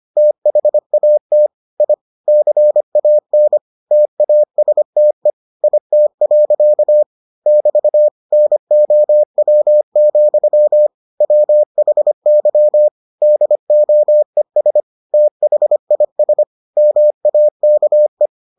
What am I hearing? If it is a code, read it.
Morse code, heard as a keyed tone, about 25 wpm, 600 Hz. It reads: THAT I CAN TASTE IT. <BT> NOW, WHY DOES THIS MAKE